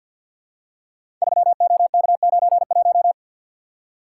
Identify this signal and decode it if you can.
Morse code, heard as a keyed tone, about 52 wpm, 700 Hz. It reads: 3OK91
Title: Code Groups - 5 Letter:Numbers - 50wpm